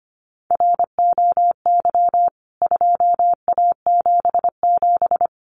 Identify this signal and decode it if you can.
Morse code, heard as a keyed tone, about 25 wpm, 700 Hz. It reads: ROY 2A77
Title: CWT Exchange 25wpm